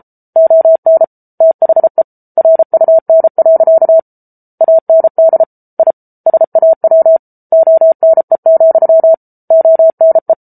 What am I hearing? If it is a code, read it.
Morse code, heard as a keyed tone, about 34 wpm, 650 Hz. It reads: ON THE RUN. AND I SAW ONE, ONE